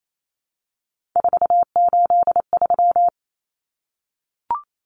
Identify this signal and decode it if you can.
Morse code, heard as a keyed tone, about 28 wpm, 700 Hz. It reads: 483
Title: Numbers - 3-Digit 28wpm